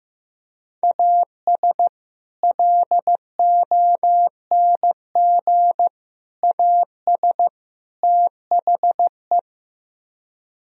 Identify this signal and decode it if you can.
Morse code, heard as a keyed tone, about 15 wpm, 700 Hz. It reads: AS LONG AS THE
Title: Speed-Racing - Sets of 4 Words 15wpm